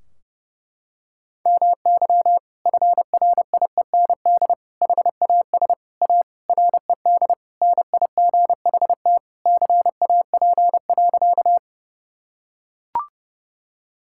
Morse code, heard as a keyed tone, about 30 wpm, 700 Hz. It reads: MY FRIEND HAS A RED NIGHT CAP.
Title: Sentences from Top 1600 Words 30wpm